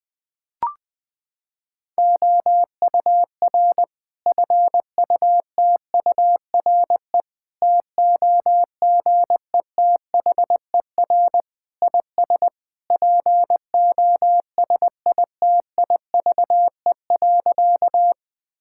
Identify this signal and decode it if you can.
Morse code, heard as a keyed tone, about 20 wpm, 700 Hz. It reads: OUR FUTURE TOGETHER IS POSITIVE.
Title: Sentences from Top 1000 Words - Review Part 5 - 20wpm